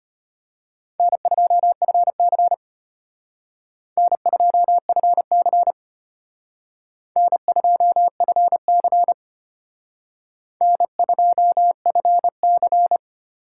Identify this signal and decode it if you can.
Morse code, heard as a keyed tone, about 32 wpm, 700 Hz. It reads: N2FC N2FC N2FC N2FC